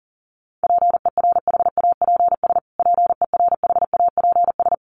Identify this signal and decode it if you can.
Morse code, heard as a keyed tone, about 40 wpm, 700 Hz. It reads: PERHAPS PERHAPS
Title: Top 500 Words Repeated 40wpm